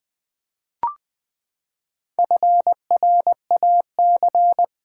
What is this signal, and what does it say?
Morse code, heard as a keyed tone, about 20 wpm, 700 Hz. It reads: FRAC
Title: Build A Word - 4 Letters At A Time - 8-Letter Words - 20wpm